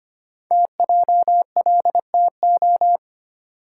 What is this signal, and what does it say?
Morse code, heard as a keyed tone, about 25 wpm, 700 Hz. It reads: TJLTO
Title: Code Groups - 5 Letters - 25wpm